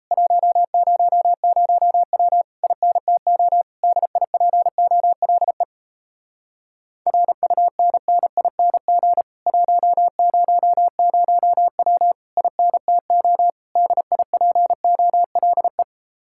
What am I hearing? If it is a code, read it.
Morse code, heard as a keyed tone, about 38 wpm, 700 Hz. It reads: 100W INTO DIPOLE RUNNING 100W INTO DIPOLE